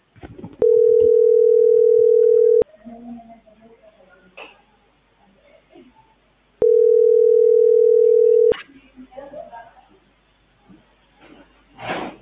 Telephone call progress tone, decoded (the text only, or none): ringback tone